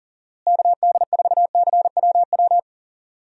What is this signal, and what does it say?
Morse code, heard as a keyed tone, about 40 wpm, 700 Hz. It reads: KD4CWW
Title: USA Call Signs - 6 Characters - Repeated 3x - 40wpm